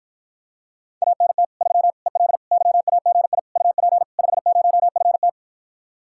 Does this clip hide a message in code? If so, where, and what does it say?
Yes: Morse code from 1.0 to 5.3 s, heard as a keyed tone, about 54 wpm, 700 Hz. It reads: ANT 3 EL YAGI UP 50FT